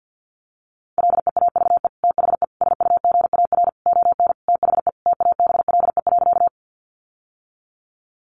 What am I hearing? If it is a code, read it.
Morse code, heard as a keyed tone, about 50 wpm, 700 Hz. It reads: LEAVE THE SUGAR ON THE TABLE.
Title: Sentences from Top 1200 Words 50wpm